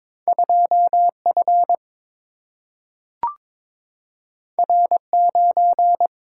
Morse code, heard as a keyed tone, about 22 wpm, 700 Hz. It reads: 2 F R 9
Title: Single-Letter Single-Number 22wpm